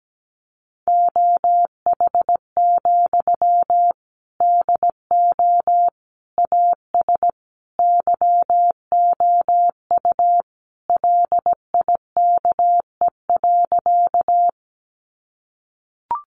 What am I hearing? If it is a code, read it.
Morse code, heard as a keyed tone, about 17 wpm, 700 Hz. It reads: OH, DO AS YOU LIKE.
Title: Sentences from Top 500 Words 17wpm